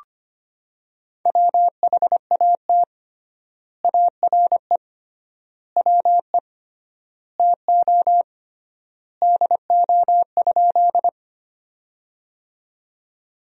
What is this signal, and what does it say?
Morse code, heard as a keyed tone, about 25 wpm, 700 Hz. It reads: WHAT ARE WE TO DO?
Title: 3x Word-Spacing - Sentences from Top 100 Words - 25wpm